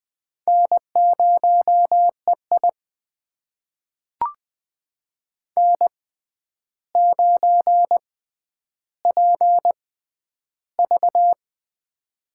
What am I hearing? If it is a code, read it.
Morse code, heard as a keyed tone, about 20 wpm, spaced as for 10 wpm, 700 Hz. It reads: N0EI N9PV
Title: USA Call Signs - 4 Characters - 6x Character Spacing - 20wpm